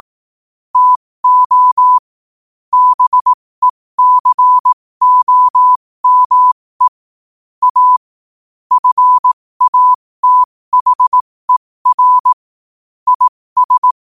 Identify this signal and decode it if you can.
Morse code, heard as a keyed tone, about 18 wpm, spaced as for 15 wpm, 1000 Hz. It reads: TO BECOME A FATHER IS
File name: qotdfiles / 15wpm